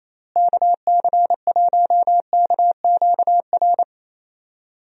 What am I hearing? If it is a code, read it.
Morse code, heard as a keyed tone, about 28 wpm, 700 Hz. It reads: KC1KQR
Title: USA Call Signs - 6 Characters - Repeated 2x - 28wpm